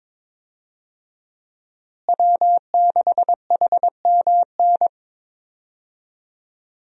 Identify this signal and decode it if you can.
Morse code, heard as a keyed tone, about 22 wpm, 700 Hz. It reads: W6HMN
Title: USA Call Signs - 5 Characters - Repeated 3x - 22wpm